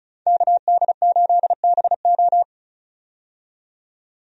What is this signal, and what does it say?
Morse code, heard as a keyed tone, about 34 wpm, 700 Hz. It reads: KD8BO
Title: USA Call Signs - 5 Characters - 35wpm